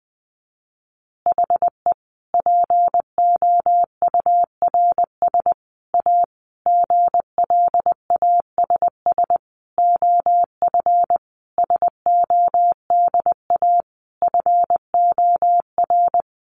Morse code, heard as a keyed tone, about 20 wpm, 700 Hz. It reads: HE POURS A GLASS OF SODA FOR
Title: Additional Easy Phrases 20wpm